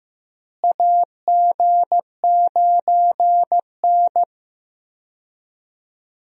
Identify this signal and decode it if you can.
Morse code, heard as a keyed tone, about 15 wpm, 700 Hz. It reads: AG9N
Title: USA Call Signs - 4 Characters - Repeated 3x - 15wpm